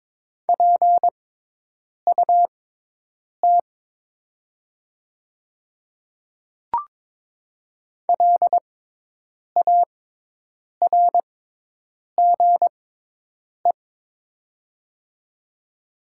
Morse code, heard as a keyed tone, about 22 wpm, spaced as for 8 wpm, 700 Hz. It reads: PUT LARGE
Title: 6x Character Spacing - Top 200 Words - 22wpm